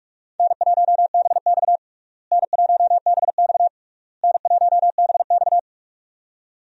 Morse code, heard as a keyed tone, about 46 wpm, 700 Hz. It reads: N1BX N1BX N1BX
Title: USA Call Signs - 4 Characters - Repeated 3x - 45wpm